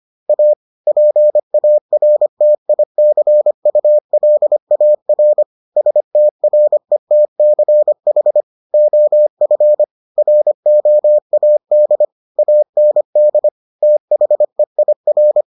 Morse code, heard as a keyed tone, about 25 wpm, 600 Hz. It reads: A PARTICULAR STRETCH OF ROAD AND THEIR